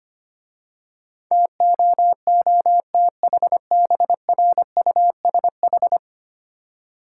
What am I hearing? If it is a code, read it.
Morse code, heard as a keyed tone, about 25 wpm, 700 Hz. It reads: TOOTHBRUSH
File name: Easy Words - Speed-Racing with Hints - 25wpm